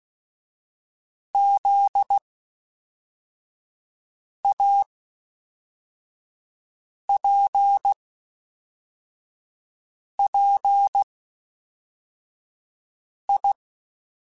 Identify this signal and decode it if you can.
Morse code, heard as a keyed tone, about 16 wpm, spaced as for 4 wpm, 800 Hz. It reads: ZAPPI
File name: Special INGs / Special INGs-mp3s / Fast ing @10